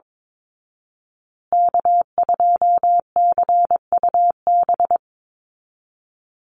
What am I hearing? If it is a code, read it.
Morse code, heard as a keyed tone, about 22 wpm, 700 Hz. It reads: K2CUB
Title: Call Signs Repeated 3x - 22wpm